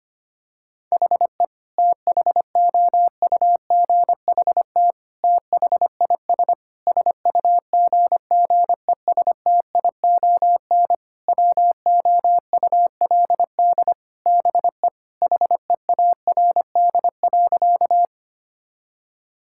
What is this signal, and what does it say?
Morse code, heard as a keyed tone, about 25 wpm, 700 Hz. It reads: HE THOUGHT THIS SUGGESTION WOULD BE HEARD.
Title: Sentences from Top 1600 Words 25wpm